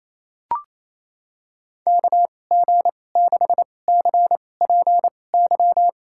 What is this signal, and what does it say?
Morse code, heard as a keyed tone, about 27 wpm, spaced as for 20 wpm, 700 Hz. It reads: KG6CPY
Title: USA Call Signs - 6 Characters - 2x Character Spacing - 28wpm